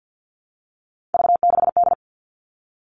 Morse code, heard as a keyed tone, about 48 wpm, 700 Hz. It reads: V6D
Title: All Cut Numbers - 3-Digit 50wpm